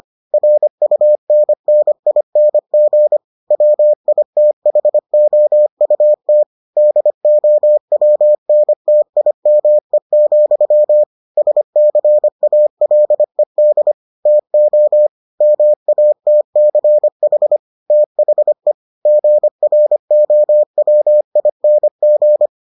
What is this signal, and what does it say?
Morse code, heard as a keyed tone, about 25 wpm, 600 Hz. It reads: RUNNING WITHOUT DOWNTIME, SCALED TO MATCH THE GROWING